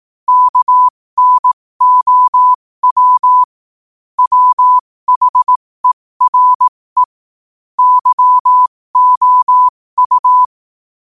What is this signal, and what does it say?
Morse code, heard as a keyed tone, about 18 wpm, spaced as for 15 wpm, 1000 Hz. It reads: KNOW WHERE YOU
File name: qotdfiles / 15wpm